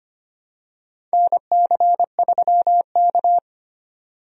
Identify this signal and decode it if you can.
Morse code, heard as a keyed tone, about 25 wpm, 700 Hz. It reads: NC3K
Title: USA Call Signs - 4 Characters - 25wpm